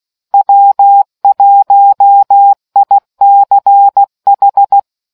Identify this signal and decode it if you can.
Morse code, heard as a keyed tone, about 16 wpm, 800 Hz. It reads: W1ICH